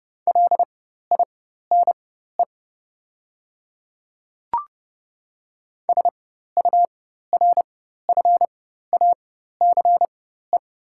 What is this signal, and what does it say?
Morse code, heard as a keyed tone, about 30 wpm, spaced as for 14 wpm, 690 Hz. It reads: LINE SURFACE
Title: 4x Character Spacing - Top 500 Words - 30wpm